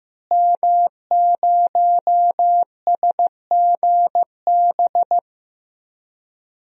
Morse code, heard as a keyed tone, about 15 wpm, 700 Hz. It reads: M0SGB
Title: DXCC-100 Call Signs - 5 Characters - Repeated 3x - 15wpm